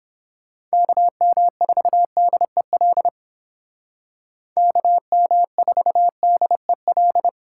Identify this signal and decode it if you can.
Morse code, heard as a keyed tone, about 30 wpm, 700 Hz. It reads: KM4DEL KM4DEL